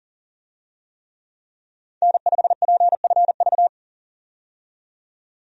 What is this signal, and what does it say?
Morse code, heard as a keyed tone, about 40 wpm, 700 Hz. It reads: N5PFV
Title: USA Call Signs - 5 Characters - 40wpm